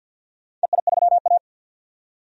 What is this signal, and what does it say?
Morse code, heard as a keyed tone, about 51 wpm, 700 Hz. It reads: EI3A
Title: DXCC-100 Call Signs - 4 Characters - 50wpm